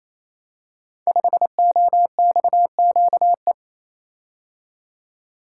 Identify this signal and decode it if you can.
Morse code, heard as a keyed tone, about 27 wpm, 700 Hz. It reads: 5OXQE
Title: Code Groups - 5 Letter:Numbers - 28wpm